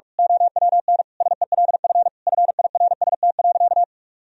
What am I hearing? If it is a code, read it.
Morse code, heard as a keyed tone, about 41 wpm, 700 Hz. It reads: OWN SELF FIRST.
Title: Sentences from Top 1000 Words - Review Part 1 - 45wpm